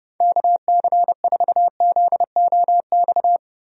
Morse code, heard as a keyed tone, about 32 wpm, 700 Hz. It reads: KC4ZOX
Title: USA Call Signs - 6 Characters - 30wpm